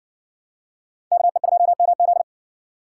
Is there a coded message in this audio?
Yes: Morse code from 1.1 to 2.2 s, heard as a keyed tone, about 57 wpm, 700 Hz. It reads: KE2MZ